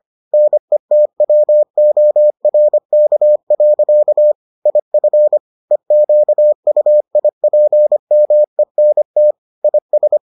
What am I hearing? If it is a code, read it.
Morse code, heard as a keyed tone, about 25 wpm, 600 Hz. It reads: NETWORK. IF EQUIPMENT IS